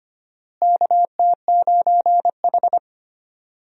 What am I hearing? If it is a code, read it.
Morse code, heard as a keyed tone, about 25 wpm, 700 Hz. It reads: KT9H
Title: USA Call Signs - 4 Characters - Repeated 3x - 25wpm